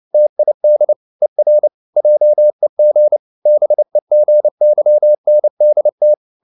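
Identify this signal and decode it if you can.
Morse code, heard as a keyed tone, about 29 wpm, 600 Hz. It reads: TID ER JEG BEGYNDT
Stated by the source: Fast (QRQ) Morse at 29 wpm, from CW med Gnister.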